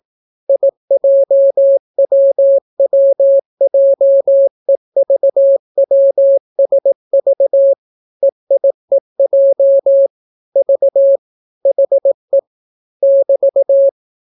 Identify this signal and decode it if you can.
Morse code, heard as a keyed tone, about 18 wpm, 550 Hz. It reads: IJWWJEVWSV EIEJ V HE <BT>